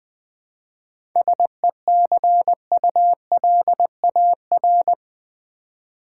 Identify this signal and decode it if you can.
Morse code, heard as a keyed tone, about 20 wpm, 700 Hz. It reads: SECULAR
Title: 7 Letter Words - First 3 Letters Given - 20wpm